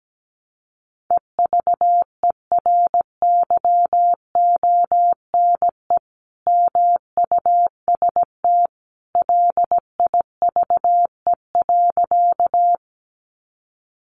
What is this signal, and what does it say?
Morse code, heard as a keyed tone, about 17 wpm, 700 Hz. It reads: EVERYONE MUST LIVE.
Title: Sentences from Top 800 Words 17wpm